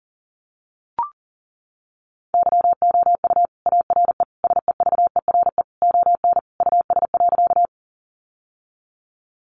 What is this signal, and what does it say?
Morse code, heard as a keyed tone, about 37 wpm, 700 Hz. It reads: YOU ARE SEVERE ON US.
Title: Sentences from Top 1400 Words 40wpm